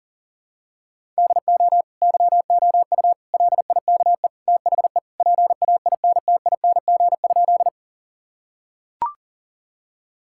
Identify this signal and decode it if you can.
Morse code, heard as a keyed tone, about 40 wpm, 700 Hz. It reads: DO YOU LIKE THE PAINTING?
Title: Sentences from Top 1000 Words - Review Part 6 - 40wpm